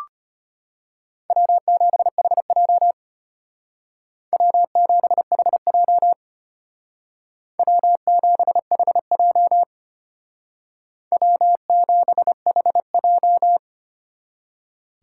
Morse code, heard as a keyed tone, about 33 wpm, 700 Hz. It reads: W7HJ W7HJ W7HJ W7HJ